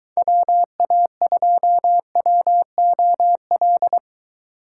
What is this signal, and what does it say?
Morse code, heard as a keyed tone, about 22 wpm, 700 Hz. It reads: WA2WOL